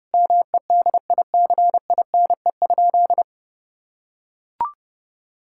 Morse code, heard as a keyed tone, about 30 wpm, 700 Hz. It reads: MEDICINE?
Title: Sentences from Top 1000 Words - Review Part 6 - 30wpm